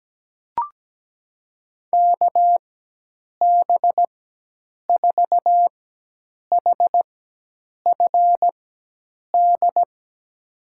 Morse code, heard as a keyed tone, about 17 wpm, spaced as for 8 wpm, 700 Hz. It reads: KB4HFD
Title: USA Call Signs - 4x Character Spacing - 17wpm